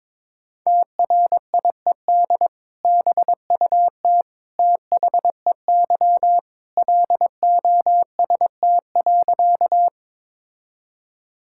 Morse code, heard as a keyed tone, about 22 wpm, 700 Hz. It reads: TRIED BUT THEY LOST.
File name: practice / 22wpm / Sentences with 2-Word Phrases - 22wpm